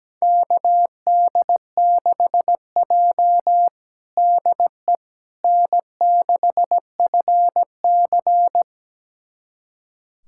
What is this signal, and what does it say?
Morse code, heard as a keyed tone, about 17 wpm, 700 Hz. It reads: KD6J DE N6FC
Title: USA Callsign DE Callsign - 4 Characters - 17wpm